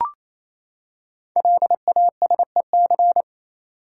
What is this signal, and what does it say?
Morse code, heard as a keyed tone, about 28 wpm, 690 Hz. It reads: LASEC
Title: Code Groups - 5 Letters - 28wpm